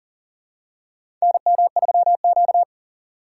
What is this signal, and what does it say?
Morse code, heard as a keyed tone, about 40 wpm, 700 Hz. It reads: NM3Q
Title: USA Call Signs - 2x Character Spacing - 40wpm